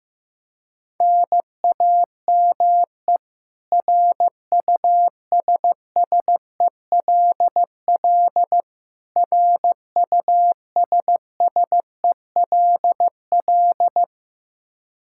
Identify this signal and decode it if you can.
Morse code, heard as a keyed tone, about 15 wpm, 700 Hz. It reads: NAME RUSSELL RUSSELL
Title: K1USN SST QSO Practice 15wpm